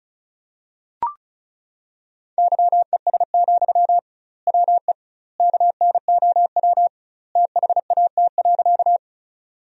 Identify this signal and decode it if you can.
Morse code, heard as a keyed tone, about 36 wpm, 700 Hz. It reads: YES, WE KNOW THAT.
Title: 2x Word-Spacing - Sentences from Top 500 Words - 35wpm